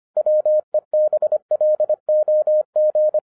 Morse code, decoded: WEBLOG

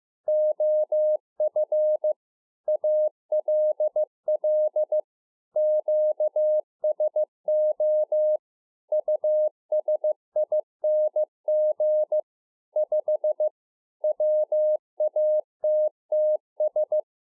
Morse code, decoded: OF ALL QSO USING 5 WATTS

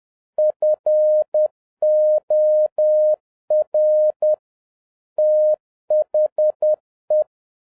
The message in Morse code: FOR THE

